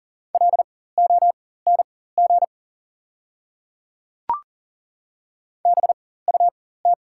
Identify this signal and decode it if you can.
Morse code, heard as a keyed tone, about 37 wpm, spaced as for 18 wpm, 700 Hz. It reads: LONG BUT
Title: 4x Character Spacing - Top 100 Words - 40wpm